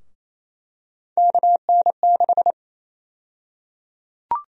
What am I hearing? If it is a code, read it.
Morse code, heard as a keyed tone, about 28 wpm, 700 Hz. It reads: KN6
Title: USA Call Sign Prefixes - 28wpm